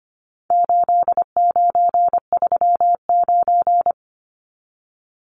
Morse code, heard as a keyed tone, about 25 wpm, 700 Hz. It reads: 8939
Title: Numbers - 4-Digit 25wpm